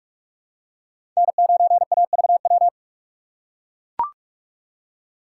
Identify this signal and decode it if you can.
Morse code, heard as a keyed tone, about 41 wpm, 700 Hz. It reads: N9AVW
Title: USA Call Signs - 5 Characters - 4x Character Spacing - 45wpm